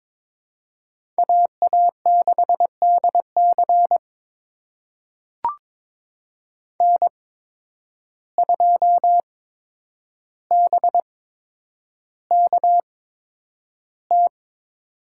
Morse code, heard as a keyed tone, about 22 wpm, spaced as for 9 wpm, 700 Hz. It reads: AA6DC N2BKT
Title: USA Call Signs - 5 Characters - 8x Character Spacing - 22wpm